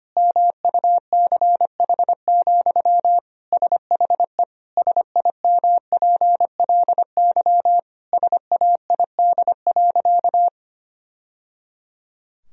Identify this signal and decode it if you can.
Morse code, heard as a keyed tone, about 25 wpm, 700 Hz. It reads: MUCH, SHE SIMPLY SAID.
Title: Sentences from Top 1000 Words - Review Part 4 - 25wpm